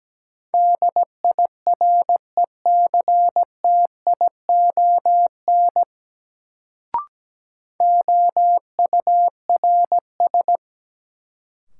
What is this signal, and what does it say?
Morse code, heard as a keyed tone, about 17 wpm, 700 Hz. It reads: DIRECTION OURS
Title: Sentences from Top 1000 Words - Review Part 5 - New Word Drill - 17wpm